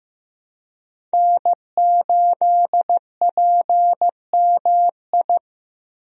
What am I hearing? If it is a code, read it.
Morse code, heard as a keyed tone, about 16 wpm, 700 Hz. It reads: N8PMI
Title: USA Call Signs - 5 Characters - Repeated 3x - 15wpm